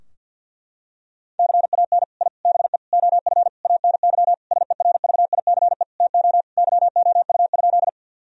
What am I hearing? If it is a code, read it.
Morse code, heard as a keyed tone, about 50 wpm, 700 Hz. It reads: CAN I BE OF ANY SERVICE TO YOU?
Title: Sentences from Top 1000 Words - Review Part 4 - 50wpm